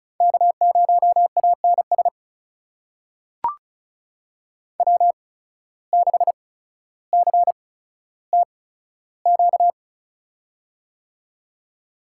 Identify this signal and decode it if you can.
Morse code, heard as a keyed tone, about 36 wpm, spaced as for 15 wpm, 700 Hz. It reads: K0ANS W6CTQ
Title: USA Call Signs - 5 Characters - 8x Character Spacing - 35wpm